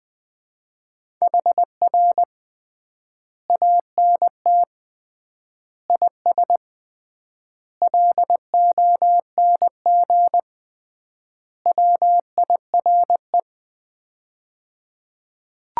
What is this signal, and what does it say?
Morse code, heard as a keyed tone, about 21 wpm, 700 Hz. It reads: HR ANT IS LONG WIRE
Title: Common QSO Phrases - 3x Word-Spacing - 20wpm